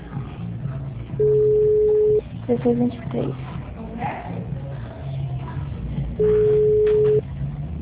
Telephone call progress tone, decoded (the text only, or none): ringback tone